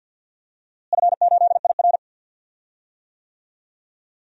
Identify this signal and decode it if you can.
Morse code, heard as a keyed tone, about 51 wpm, 700 Hz. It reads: F8IR
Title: DXCC-100 Call Signs - 4 Characters - Repeated 3x - 50wpm